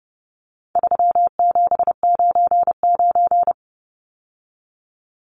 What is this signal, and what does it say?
Morse code, heard as a keyed tone, about 30 wpm, 700 Hz. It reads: 3799
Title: Numbers - 4-Digit 30wpm